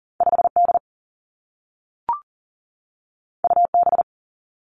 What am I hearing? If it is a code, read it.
Morse code, heard as a keyed tone, about 39 wpm, 700 Hz. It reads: 5D UB
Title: Two Challenging Letter-Number Combinations 40wpm